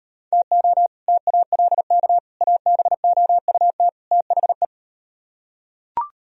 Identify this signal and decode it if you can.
Morse code, heard as a keyed tone, about 38 wpm, 700 Hz. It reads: TO TALK ABOUT THE